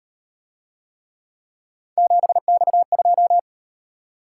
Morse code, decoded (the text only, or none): 7X2